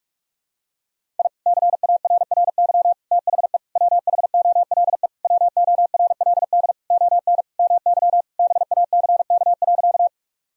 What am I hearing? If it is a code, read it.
Morse code, heard as a keyed tone, about 43 wpm, 700 Hz. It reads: I CARRY THE WHOLE WORLD ON MY BACK.
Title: Sentences from Top 1000 Words - Review Part 1 - 45wpm